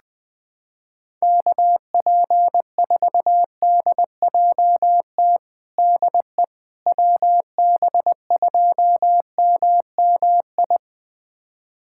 Morse code, heard as a keyed tone, about 21 wpm, 700 Hz. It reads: KP4DJT DE WB2MMI
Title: USA Callsign DE Callsign - 6 Characters - 20wpm